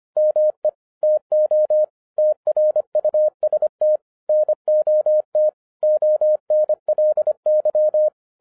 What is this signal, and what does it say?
Morse code, heard as a keyed tone, about 25 wpm, 610 Hz. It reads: ME TO TRUST NOT ONLY
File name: 2011-07-mp3 / n39 THE 3.11 2 25wpm